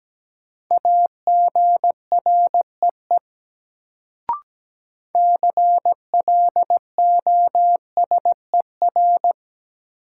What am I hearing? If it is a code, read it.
Morse code, heard as a keyed tone, about 17 wpm, 700 Hz. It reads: AGREE CLOSER
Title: Sets of 5 Words Encore - New Word Drill - 17wpm